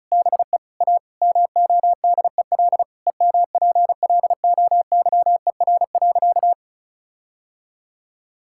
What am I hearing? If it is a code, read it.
Morse code, heard as a keyed tone, about 36 wpm, 700 Hz. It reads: BE A MODEL EMPLOYER.
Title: Sentences from Top 1600 Words 35wpm